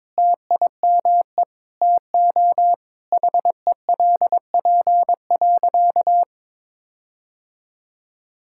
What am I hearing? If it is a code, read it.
Morse code, heard as a keyed tone, about 22 wpm, 700 Hz. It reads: TIME TO HELP.
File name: Sentences with 9-Word Phrases - 22wpm